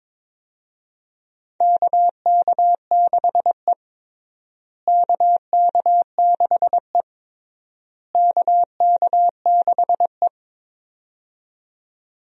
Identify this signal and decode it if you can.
Morse code, heard as a keyed tone, about 21 wpm, 700 Hz. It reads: KK6E KK6E KK6E
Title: USA Call Signs - 4 Characters - Repeated 3x - 22wpm